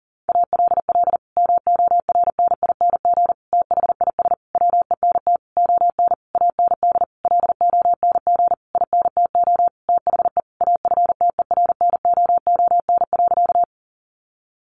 Morse code, heard as a keyed tone, about 40 wpm, 700 Hz. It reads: ALL MORNING THIS WENT ON AND LONG INTO THE AFTERNOON.
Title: Sentences from Top 1500 Words 40wpm